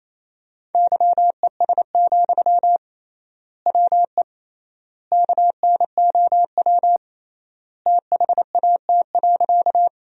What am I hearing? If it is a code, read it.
Morse code, heard as a keyed tone, about 28 wpm, 700 Hz. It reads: YES, WE KNOW THAT.
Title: 3x Word-Spacing - Sentences from Top 500 Words - 28wpm